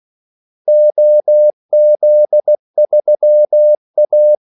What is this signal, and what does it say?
Morse code, heard as a keyed tone, about 16 wpm, 600 Hz. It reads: OZ3A
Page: Kaldesignaler 16 wpm